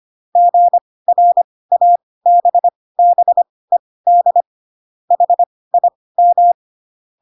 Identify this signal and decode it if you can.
Morse code, heard as a keyed tone, about 25 wpm, spaced as for 18 wpm, 700 Hz. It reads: GRABBED HIM
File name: Pinocchio ch3 12 25 ik1hge